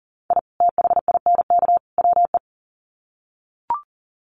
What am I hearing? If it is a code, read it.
Morse code, heard as a keyed tone, about 40 wpm, 700 Hz. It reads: I THINK WE
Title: Sets of 3 Words 40wpm